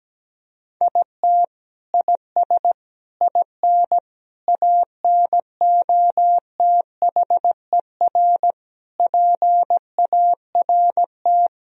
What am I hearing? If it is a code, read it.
Morse code, heard as a keyed tone, about 17 wpm, 700 Hz. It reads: IT IS IN ANOTHER PART
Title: Sentences from Top 1000 Words - Review Part 2 - 17wpm